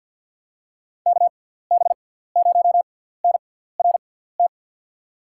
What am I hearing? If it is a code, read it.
Morse code, heard as a keyed tone, about 49 wpm, spaced as for 17 wpm, 700 Hz. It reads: KB0NRT
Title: USA Call Signs - 6x Character Spacing - 50wpm